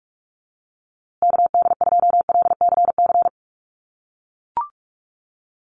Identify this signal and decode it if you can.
Morse code, heard as a keyed tone, about 44 wpm, 700 Hz. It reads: KD2LCC
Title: USA Call Signs Repeated 2x 45wpm